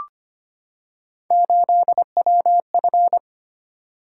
Morse code, heard as a keyed tone, about 25 wpm, 700 Hz. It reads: 8WF
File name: Three Letter-Number Combinations - 25wpm